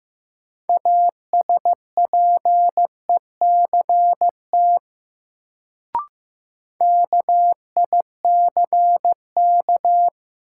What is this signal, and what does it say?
Morse code, heard as a keyed tone, about 15 wpm, 700 Hz. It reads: ASPECT KICK
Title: Sentences from Top 1300 Words - New Word Drill - 15wpm